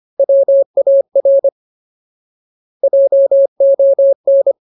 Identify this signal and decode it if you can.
Morse code, heard as a keyed tone, about 25 wpm, 550 Hz. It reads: WAR JON
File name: Ultimate Word List / 25WPM Ultimate